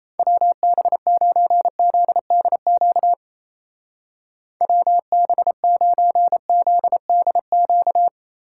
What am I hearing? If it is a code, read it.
Morse code, heard as a keyed tone, about 34 wpm, 700 Hz. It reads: WB9ZDQ WB9ZDQ
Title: Speed Racing - USA Call Signs 28wpm